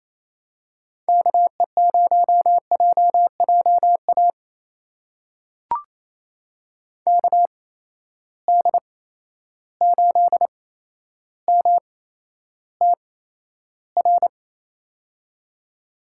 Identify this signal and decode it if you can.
Morse code, heard as a keyed tone, about 27 wpm, spaced as for 12 wpm, 700 Hz. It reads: KE0JJA KD8MTR
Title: USA Call Signs - 8x Character Spacing - 28wpm